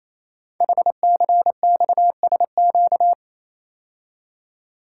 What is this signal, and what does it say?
Morse code, heard as a keyed tone, about 29 wpm, 700 Hz. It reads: HCXSQ
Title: Code Groups - 5 Letters - 28wpm